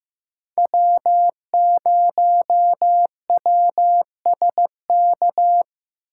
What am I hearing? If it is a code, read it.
Morse code, heard as a keyed tone, about 15 wpm, 700 Hz. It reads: W0WSK
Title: USA Call Signs - 5 Characters - 4x Character Spacing - 15wpm